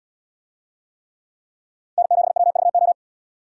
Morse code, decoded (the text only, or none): N7WVO